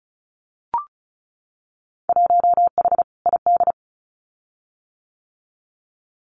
Morse code, heard as a keyed tone, about 36 wpm, 700 Hz. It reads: 1H ID
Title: Winter Field Day Exchanges 35wpm